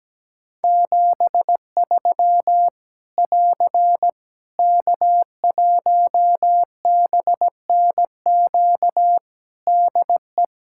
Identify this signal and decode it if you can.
Morse code, heard as a keyed tone, about 17 wpm, 700 Hz. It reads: 73 <AR> K1BNQ DE
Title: Practice QSO - Exchange 4 - Ending - 17wpm